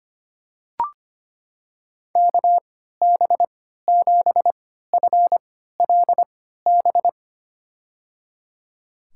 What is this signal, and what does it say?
Morse code, heard as a keyed tone, about 25 wpm, spaced as for 14 wpm, 700 Hz. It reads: KB7FLB
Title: USA Call Signs - 3x Character Spacing - 25wpm